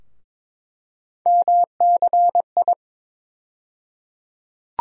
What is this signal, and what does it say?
Morse code, heard as a keyed tone, about 21 wpm, 700 Hz. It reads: MCI
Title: USA Call Sign Postfixes - 3 Letters 22wpm